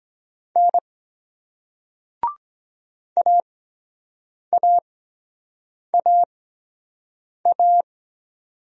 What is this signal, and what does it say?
Morse code, heard as a keyed tone, about 23 wpm, spaced as for 7 wpm, 700 Hz. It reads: N AAAA